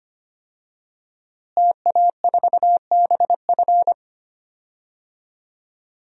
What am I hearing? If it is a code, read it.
Morse code, heard as a keyed tone, about 25 wpm, 700 Hz. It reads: TA4BF
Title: DXCC-100 Call Signs - 5 Characters - Repeated 2x - 25wpm